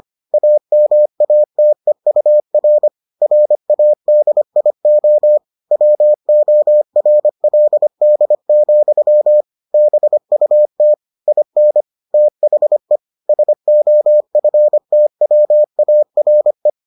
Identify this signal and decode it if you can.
Morse code, heard as a keyed tone, about 25 wpm, 600 Hz. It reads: AMATEUR RADIO WORLD, BUT IN THE SOFTWARE